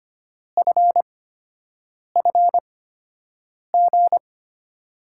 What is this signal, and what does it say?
Morse code, heard as a keyed tone, about 25 wpm, spaced as for 7 wpm, 700 Hz. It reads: FFG
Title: USA Call Signs - 6 Characters - 8x Character Spacing - 25wpm